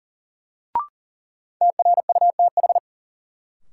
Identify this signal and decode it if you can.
Morse code, heard as a keyed tone, about 40 wpm, 700 Hz. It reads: TRUTH
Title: Sets of 5 Words - New Word Drill - 40wpm